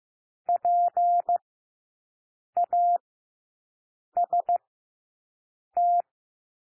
Morse code, heard as a keyed tone, about 15 wpm, spaced as for 6 wpm, 700 Hz. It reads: PAST
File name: cwmp3 / W1 SOUND WORDS 15 wpm